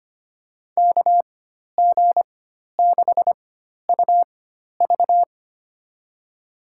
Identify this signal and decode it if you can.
Morse code, heard as a keyed tone, about 25 wpm, spaced as for 12 wpm, 700 Hz. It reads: KG6UV
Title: USA Call Signs - 5 Characters - 4x Character Spacing - 25wpm